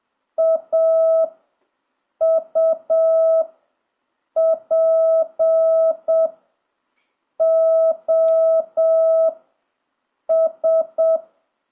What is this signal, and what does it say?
Morse code, heard as a keyed tone, about 7 wpm, 1300 Hz. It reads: A U P O S
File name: jh8cbh / 2026cw